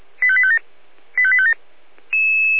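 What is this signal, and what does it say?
Five-tone selective call (ZVEI), heard as quick runs of five tones, tones of about 70 ms. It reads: 85857 85857
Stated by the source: ZVEI five-tone selective call 85810, received with an SDR by Feuerwehr Orlamünde.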